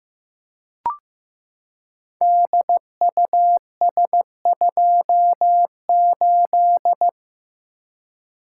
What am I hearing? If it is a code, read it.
Morse code, heard as a keyed tone, about 15 wpm, 700 Hz. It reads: DUS28
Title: Code Groups - 5 Letter:Numbers - 15wpm